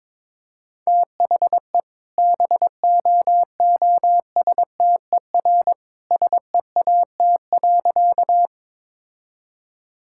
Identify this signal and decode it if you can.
Morse code, heard as a keyed tone, about 22 wpm, 690 Hz. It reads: THE BOOSTER SEAT.
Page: morse-code-ninja-practice-22wpm